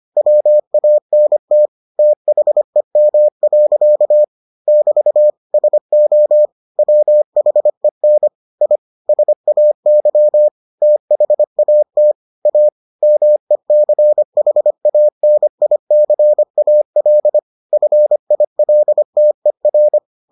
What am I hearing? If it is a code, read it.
Morse code, heard as a keyed tone, about 25 wpm, 600 Hz. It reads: WANT THEM. <BT> SO WHEN I SAY THAT A MECHANICAL FILTER